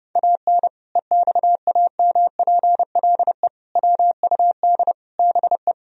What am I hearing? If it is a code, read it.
Morse code, heard as a keyed tone, about 32 wpm, 700 Hz. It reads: AN EXAMPLE WUD BE
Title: Sentences from Top 400 Words and Abbreviations 30wpm